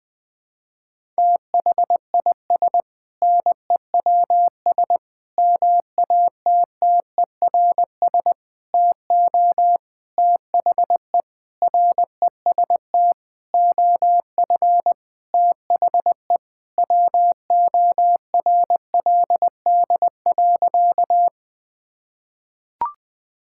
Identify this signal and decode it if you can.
Morse code, heard as a keyed tone, about 20 wpm, 700 Hz. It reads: THIS NEWS MATTERS TO THE REST OF THE WORLD.
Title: Sentences with 6-Word Phrases - Spoken Beginning - 20wpm